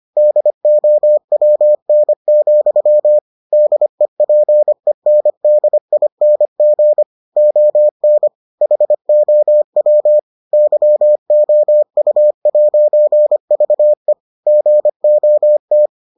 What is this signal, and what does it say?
Morse code, heard as a keyed tone, about 25 wpm, 600 Hz. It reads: DOWN, DEPENDING ON HOW YOU'VE GOT